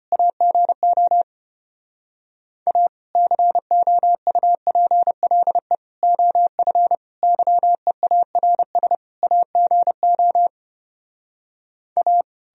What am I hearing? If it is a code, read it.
Morse code, heard as a keyed tone, about 33 wpm, 700 Hz. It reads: AGO A COUPLE OF YEARS AGO A